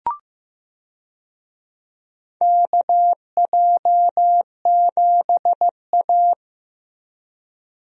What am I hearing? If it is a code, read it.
Morse code, heard as a keyed tone, about 15 wpm, 700 Hz. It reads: KJ7A
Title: USA Call Signs - 4 Characters - Repeated 3x - 15wpm